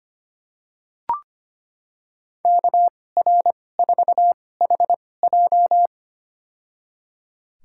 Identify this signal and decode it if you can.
Morse code, heard as a keyed tone, about 25 wpm, spaced as for 18 wpm, 700 Hz. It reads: KR4HJ
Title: USA Call Signs - 5 Characters - 2x Character Spacing - 25wpm